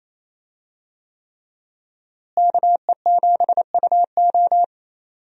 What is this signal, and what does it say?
Morse code, heard as a keyed tone, about 27 wpm, 700 Hz. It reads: KE7UO
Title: USA Call Signs - 5 Characters - Repeated 3x - 28wpm